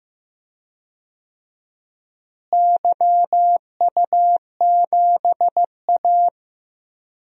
Morse code, heard as a keyed tone, about 15 wpm, 700 Hz. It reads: YU7A
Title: DXCC-100 Call Signs - 4 Characters - Repeated 3x - 15wpm